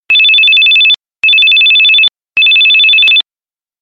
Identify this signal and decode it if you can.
There is no code.